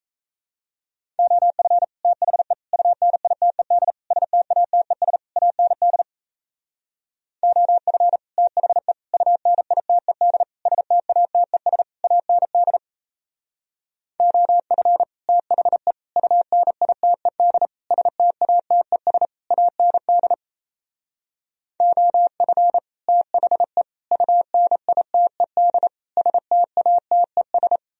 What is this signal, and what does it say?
Morse code, heard as a keyed tone, about 37 wpm, 700 Hz. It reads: OF THE UNITED STATES AND OF THE UNITED STATES AND OF THE UNITED STATES AND OF THE UNITED STATES